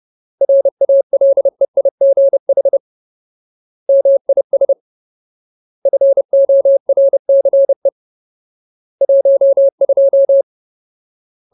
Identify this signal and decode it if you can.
Morse code, heard as a keyed tone, about 30 wpm, 550 Hz. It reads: RALEIGH MIS FORCE 12
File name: Ultimate Word List / 30WPM Ultimate